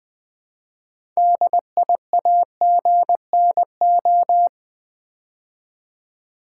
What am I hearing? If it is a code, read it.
Morse code, heard as a keyed tone, about 20 wpm, 700 Hz. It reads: DIAGNO
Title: Build A Word - 3 Letters At A Time - 9-Letter Words - 20wpm